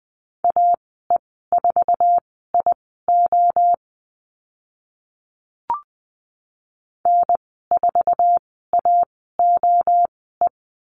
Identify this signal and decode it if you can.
Morse code, heard as a keyed tone, about 20 wpm, spaced as for 14 wpm, 700 Hz. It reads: AE4IO N4AOE
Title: 082 Lesson 14g - Number 2 - Call Signs 20wpm